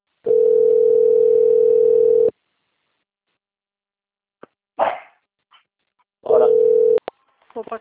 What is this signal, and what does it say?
Ringback tone (North America), 440 + 480 Hz.